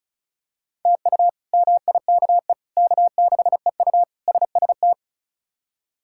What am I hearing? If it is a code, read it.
Morse code, heard as a keyed tone, about 36 wpm, 700 Hz. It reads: TU MIKE K6EU SST
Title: K1USN SST QSO Practice - 35wpm